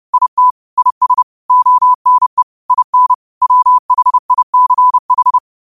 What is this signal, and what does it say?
Morse code, heard as a keyed tone, about 30 wpm, 1000 Hz. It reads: IT IS ONE IN WHICH
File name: qotdfiles / 30wpm